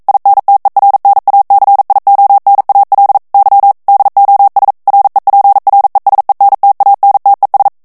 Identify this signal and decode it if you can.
Morse code, heard as a keyed tone, about 42 wpm, 800 Hz. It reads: INTERNACIONAL Y DOS REPRESENTANTES
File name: morse mp3 / ESPAGNOL TEXTE EN CLAIR 1 VITESSE 40 MOTS MINUTES